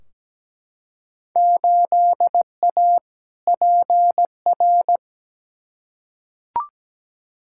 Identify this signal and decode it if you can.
Morse code, heard as a keyed tone, about 17 wpm, 700 Hz. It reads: 8A PR
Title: Field Day Exchanges Encore - 17wpm